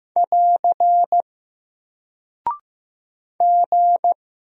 Morse code, heard as a keyed tone, about 15 wpm, 700 Hz. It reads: <AR> G
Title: Single Character (Letter:Number:Punctuation:Prosign) Alt 15wpm